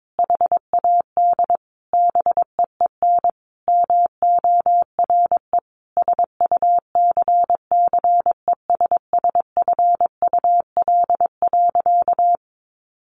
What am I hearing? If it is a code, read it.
Morse code, heard as a keyed tone, about 22 wpm, 700 Hz. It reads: HAD BEEN MORE SUCCESSFUL.
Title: Sentences from Top 1500 Words 22wpm